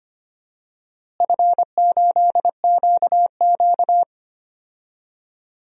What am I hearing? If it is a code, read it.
Morse code, heard as a keyed tone, about 25 wpm, 700 Hz. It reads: F8QQ
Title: DXCC-100 Call Signs - 4 Characters - Repeated 3x - 25wpm